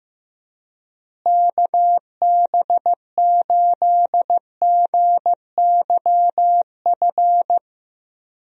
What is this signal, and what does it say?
Morse code, heard as a keyed tone, about 15 wpm, 700 Hz. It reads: KB8GYF
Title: USA Call Signs - 6 Characters - Repeated 3x - 15wpm